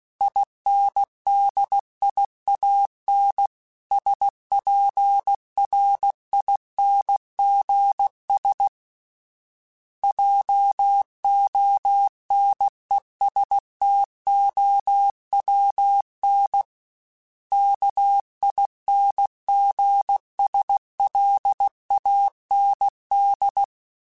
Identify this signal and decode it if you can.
Morse code, heard as a keyed tone, about 16 wpm, 800 Hz. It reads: INDIAN SPRINGS JONESTOWN KINGSLAND
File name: fromSMs / FromSMs mp3s / Cities@15